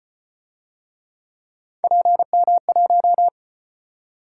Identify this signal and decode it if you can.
Morse code, heard as a keyed tone, about 34 wpm, 700 Hz. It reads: PM1